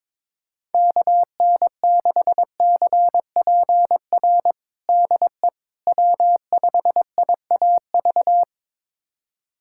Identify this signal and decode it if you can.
Morse code, heard as a keyed tone, about 22 wpm, 700 Hz. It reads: KN6CPR DE W5IAV
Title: USA Callsign DE Callsign 22wpm